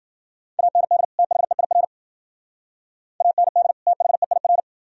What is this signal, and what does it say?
Morse code, heard as a keyed tone, about 50 wpm, 700 Hz. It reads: AND THEIR AND THEIR
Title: Speed-Racing - Sets of 2 Words 40wpm